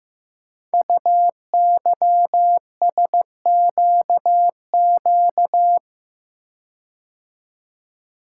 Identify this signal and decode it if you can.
Morse code, heard as a keyed tone, about 15 wpm, 700 Hz. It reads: UYSQQ
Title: Code Groups - 5 Letters - 15wpm